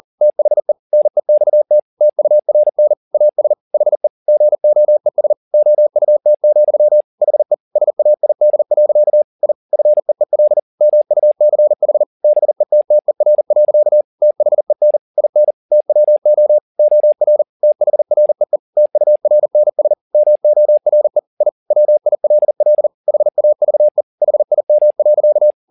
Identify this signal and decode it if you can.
Morse code, heard as a keyed tone, about 40 wpm, 600 Hz. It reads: THE NEXT TURN AS HE GOES OUT, HE SAID. I FEEL MUCH BETTER. THEN IN TWO OR THREE TURNS MORE I WILL HAVE HIM.
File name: the old man and the sea 2 40wpm